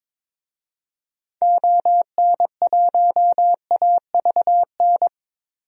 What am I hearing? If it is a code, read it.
Morse code, heard as a keyed tone, about 22 wpm, 700 Hz. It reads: ON1AVN
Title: DXCC-100 Call Signs - 6 Characters - 22wpm